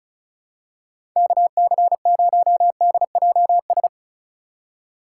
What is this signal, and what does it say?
Morse code, heard as a keyed tone, about 36 wpm, 700 Hz. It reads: KC0DJS
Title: USA Call Signs - 6 Characters - 4x Character Spacing - 35wpm